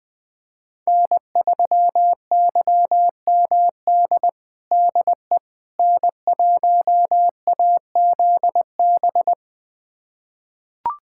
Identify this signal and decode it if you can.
Morse code, heard as a keyed tone, about 20 wpm, 700 Hz. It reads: N3YMD DE N1AZB
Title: USA Callsign DE Callsign - 5 Characters - 20wpm